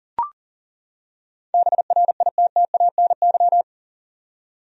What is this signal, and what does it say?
Morse code, heard as a keyed tone, about 40 wpm, 700 Hz. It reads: BRITTANY
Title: USA Common Names - 40wpm